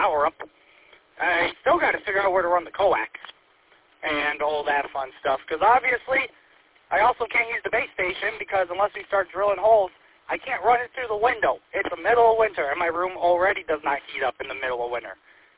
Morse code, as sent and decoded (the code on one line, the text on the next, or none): none